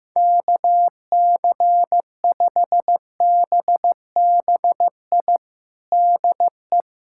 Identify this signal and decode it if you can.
Morse code, heard as a keyed tone, about 15 wpm, 700 Hz. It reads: KC5BBI DE
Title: USA Callsign DE Callsign - 6 Characters - 15wpm